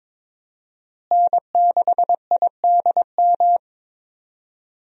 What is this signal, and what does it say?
Morse code, heard as a keyed tone, about 22 wpm, 700 Hz. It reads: N6IDM
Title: USA Call Signs - 5 Characters - Repeated 3x - 22wpm